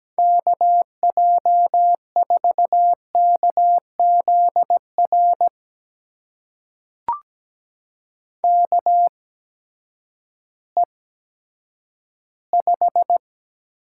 Morse code, heard as a keyed tone, about 17 wpm, spaced as for 10 wpm, 700 Hz. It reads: KJ4KZR KE5
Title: USA Call Signs - 6 Characters - 8x Character Spacing - 17wpm